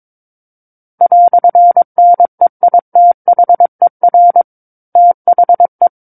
Morse code, heard as a keyed tone, about 22 wpm, 700 Hz. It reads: "NEITHER THE
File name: Book One / 13-Chapter 13 - 22 WPM